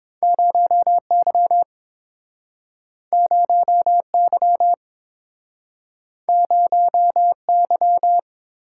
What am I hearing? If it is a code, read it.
Morse code, heard as a keyed tone, about 30 wpm, 700 Hz. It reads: 0Y 0Y 0Y